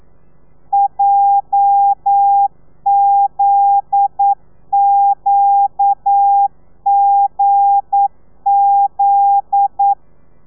Morse code, heard as a keyed tone, about 9 wpm, 800 Hz. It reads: JZQGZ
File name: telegrafia / i2tfj / Lez9 40